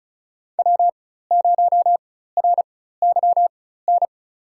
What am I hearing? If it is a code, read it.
Morse code, heard as a keyed tone, about 34 wpm, spaced as for 16 wpm, 700 Hz. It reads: W0RYN
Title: USA Call Signs - 5 Characters - 4x Character Spacing - 35wpm